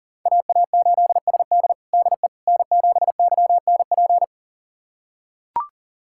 Morse code, heard as a keyed tone, about 40 wpm, 700 Hz. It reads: AA8SD DE N7YNP
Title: USA Callsign DE Callsign - 5 Characters - 40wpm